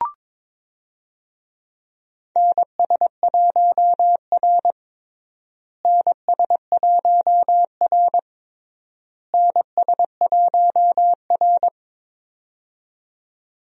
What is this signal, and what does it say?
Morse code, heard as a keyed tone, about 22 wpm, 700 Hz. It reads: NS1R NS1R NS1R
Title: USA Call Signs - 4 Characters - Repeated 3x - 22wpm